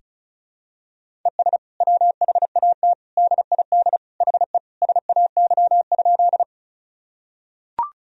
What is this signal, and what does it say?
Morse code, heard as a keyed tone, about 36 wpm, 700 Hz. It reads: ES WHAT DID HE SAY?
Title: Sentences from Top 200 Words and Abbreviations 35wpm